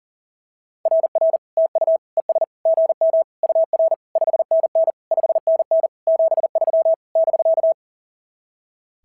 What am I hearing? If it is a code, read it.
Morse code, heard as a keyed tone, about 39 wpm, 660 Hz. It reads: RR TU ES GM UR 5NN 5NN 73 <BK>
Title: SOTA QSO Practice 40wpm